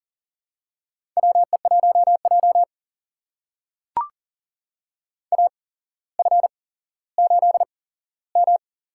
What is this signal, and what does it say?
Morse code, heard as a keyed tone, about 38 wpm, spaced as for 17 wpm, 700 Hz. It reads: WE1J AF8M
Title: USA Call Signs - 4 Characters - 8x Character Spacing - 40wpm